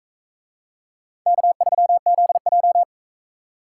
Morse code, heard as a keyed tone, about 44 wpm, 700 Hz. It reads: K3ZJ